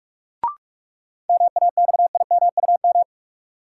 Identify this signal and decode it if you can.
Morse code, heard as a keyed tone, about 41 wpm, 700 Hz. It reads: MAXIMUM
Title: Sentences from Top 1300 Words - New Word Drill - 45wpm